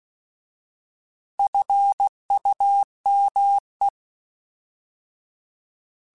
Morse code, heard as a keyed tone, about 16 wpm, 800 Hz. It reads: FUME
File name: Alphabetical / Alphabetical mp3s / Fs @15